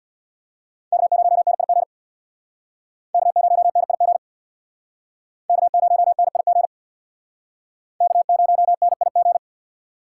Morse code, heard as a keyed tone, about 64 wpm, 700 Hz. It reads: K0NIG K0NIG K0NIG K0NIG